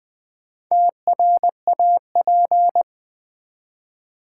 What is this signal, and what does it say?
Morse code, heard as a keyed tone, about 20 wpm, 700 Hz. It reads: TRAP
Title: Build A Word - 2 Letters At A Time - 4-Letter Words - 20wpm